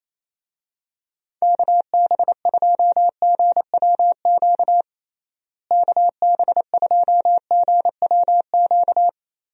Morse code, heard as a keyed tone, about 27 wpm, 700 Hz. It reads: KB2GWQ KB2GWQ
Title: USA Call Signs - 6 Characters - Repeated 3x - 28wpm